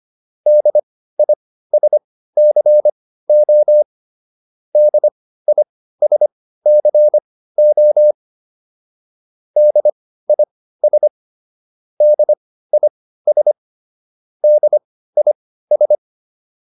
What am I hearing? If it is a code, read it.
Morse code, heard as a keyed tone, about 25 wpm, spaced as for 15 wpm, 600 Hz. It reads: DISCO DISCO DIS DIS DIS